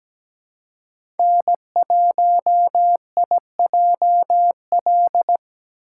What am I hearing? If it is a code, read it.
Morse code, heard as a keyed tone, about 17 wpm, 700 Hz. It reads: N1IJL
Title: USA Callsign DE Callsign - 5 Characters - 17wpm